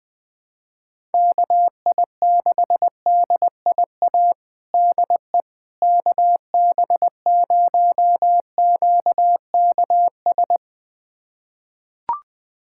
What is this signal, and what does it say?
Morse code, heard as a keyed tone, about 20 wpm, 700 Hz. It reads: KI6DIA DE KB0QKS
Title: USA Callsign DE Callsign 20wpm